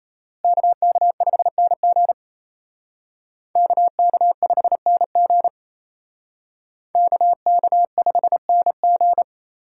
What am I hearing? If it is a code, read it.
Morse code, heard as a keyed tone, about 32 wpm, 700 Hz. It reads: KK5NG KK5NG KK5NG
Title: Speed Racing - USA 2x2 Call Signs 28wpm